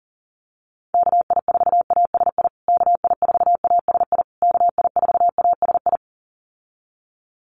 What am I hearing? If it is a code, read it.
Morse code, heard as a keyed tone, about 40 wpm, 700 Hz. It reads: KI4ASI KI4ASI KI4ASI
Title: USA Call Signs Repeated 3x 40wpm